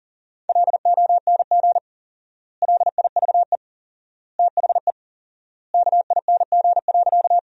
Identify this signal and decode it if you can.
Morse code, heard as a keyed tone, about 40 wpm, 700 Hz. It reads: LONG LIVE THE KING.
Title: 4x Word-Spacing - Sentences from Top 400 Words - 40wpm